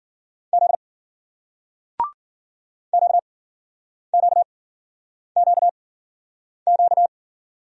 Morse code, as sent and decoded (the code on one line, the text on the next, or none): --.. / --.- --.- --.- --.-
Z QQQQ